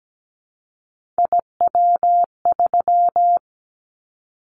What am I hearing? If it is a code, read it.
Morse code, heard as a keyed tone, about 17 wpm, 700 Hz. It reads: IW3
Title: Callsign Prefixes for Top 25 DXCC - 17wpm